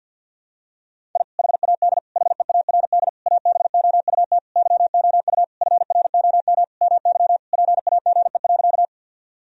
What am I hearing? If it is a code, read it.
Morse code, heard as a keyed tone, about 52 wpm, 700 Hz. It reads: I HAD HEARD ABOUT YOU FROM MY PAGE.
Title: Sentences from Top 1000 Words - Review Part 1 - 50wpm